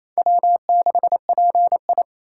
Morse code, heard as a keyed tone, about 28 wpm, 700 Hz. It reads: W6PI
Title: USA Call Signs - 4 Characters - 2x Character Spacing - 28wpm